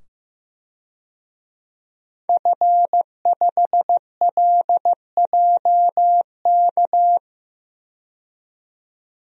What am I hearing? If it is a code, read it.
Morse code, heard as a keyed tone, about 16 wpm, 700 Hz. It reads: F5LJK
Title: DXCC-100 Call Signs - 5 Characters - 15wpm